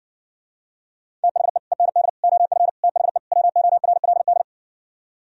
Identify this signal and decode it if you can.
Morse code, heard as a keyed tone, about 60 wpm, 700 Hz. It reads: THE END OF THE WORLD